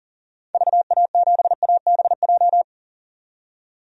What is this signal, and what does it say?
Morse code, heard as a keyed tone, about 39 wpm, 700 Hz. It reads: VA7ABJ
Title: DXCC-100 Call Signs - 6 Characters - 40wpm